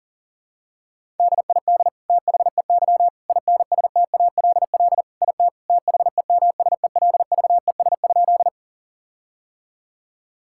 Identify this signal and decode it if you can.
Morse code, heard as a keyed tone, about 40 wpm, 700 Hz. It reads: DID THEY INSTALL IT THEMSELVES?
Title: Sentences from Top 1600 Words 40wpm